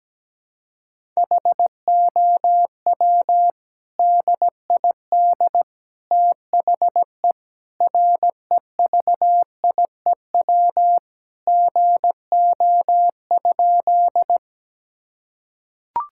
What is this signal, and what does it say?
Morse code, heard as a keyed tone, about 17 wpm, 700 Hz. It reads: HOW DID THE REVIEW GO?
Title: Sentences from Top 1000 Words - Review Part 6 - 17wpm